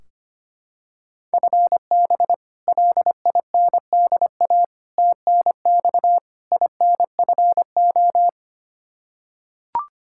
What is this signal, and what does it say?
Morse code, heard as a keyed tone, about 25 wpm, 700 Hz. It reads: FB LINDA TNX INFO
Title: Practice QSO - Exchange 3 - More Info - 25wpm